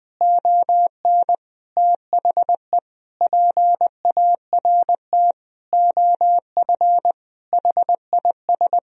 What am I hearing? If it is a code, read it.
Morse code, heard as a keyed tone, about 21 wpm, 700 Hz. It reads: ON THE PART OF HIS
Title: Sentences from Top 1000 Words - Review Part 4 - 20wpm